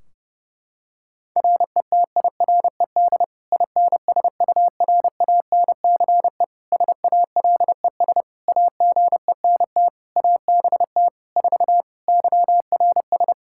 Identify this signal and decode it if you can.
Morse code, heard as a keyed tone, about 30 wpm, 700 Hz. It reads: RETIRED INSURANCE SALES AGENT ABT 4 YRS
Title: K1USN SST QSO Practice 30wpm